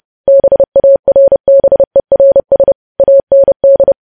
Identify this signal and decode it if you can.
Morse code, heard as a keyed tone, about 30 wpm, 550 Hz. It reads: BARBERS AND
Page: news headlines in morse code at 30 wpm